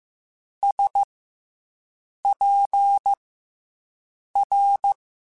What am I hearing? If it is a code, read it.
Morse code, heard as a keyed tone, about 15 wpm, 800 Hz. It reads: S P R
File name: Special INGs / Special INGs-mp3s / SlowToFaster ing @13